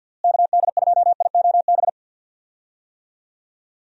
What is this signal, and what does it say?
Morse code, heard as a keyed tone, about 52 wpm, 700 Hz. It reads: KD2IOB
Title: USA Call Signs - 6 Characters - Repeated 3x - 50wpm